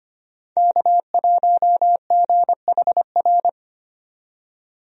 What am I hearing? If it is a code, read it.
Morse code, heard as a keyed tone, about 25 wpm, 700 Hz. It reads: K1GHR
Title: USA Call Signs - 5 Characters - 25wpm